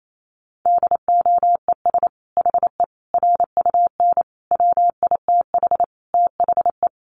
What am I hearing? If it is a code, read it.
Morse code, heard as a keyed tone, about 28 wpm, 700 Hz. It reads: DOES HE RUN WITH THE
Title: Sentences from Top 1300 Words 28wpm